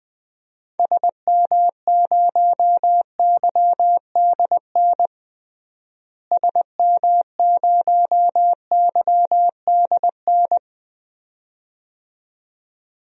Morse code, heard as a keyed tone, about 20 wpm, 700 Hz. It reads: SM0YDN SM0YDN
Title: DXCC-100 Call Signs - 6 Characters - Repeated 2x - 20wpm